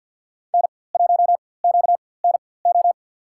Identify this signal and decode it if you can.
Morse code, heard as a keyed tone, about 53 wpm, spaced as for 24 wpm, 700 Hz. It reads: N1QNO
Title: USA Call Signs - 5 Characters - 4x Character Spacing - 50wpm